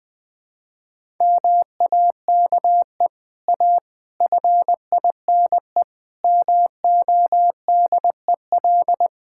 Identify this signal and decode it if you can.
Morse code, heard as a keyed tone, about 20 wpm, 700 Hz. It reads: MAKE A FINE MODEL
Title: Sentences from Top 1000 Words - Review Part 5 - 20wpm